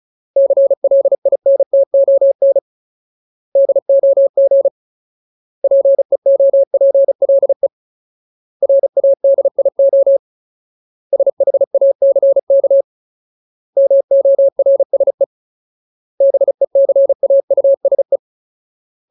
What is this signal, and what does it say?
Morse code, heard as a keyed tone, about 34 wpm, 550 Hz. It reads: CLINTON DOG PEOPLE RADIO SHACK MORSE BECAUSE
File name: Ultimate Word List / 35WPM Ultimate